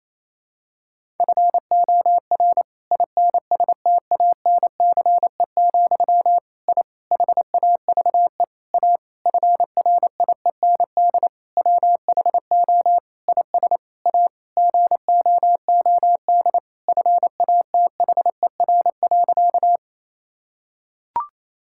Morse code, heard as a keyed tone, about 28 wpm, 700 Hz. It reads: FOR INSTANCE, I HAVE A FRIEND WHO IS A GOOD FATHER.
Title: Sentences from Top 1000 Words - Review Part 5 - 28wpm